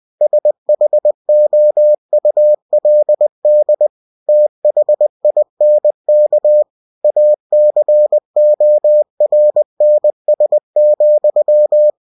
Morse code, heard as a keyed tone, about 20 wpm, 600 Hz. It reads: SHOULD THINK ACORNS,
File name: cw-books / winnie-the-pooh / winnie-20wpm-page-0037